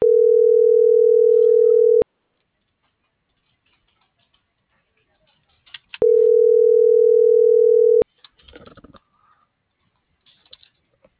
Ringback tone (North America), 440 + 480 Hz.